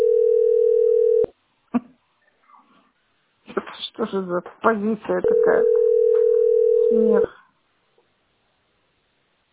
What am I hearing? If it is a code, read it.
Ringback tone (North America), 440 + 480 Hz, with speech.
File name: f86a59d1-3aa2-4ecd-b047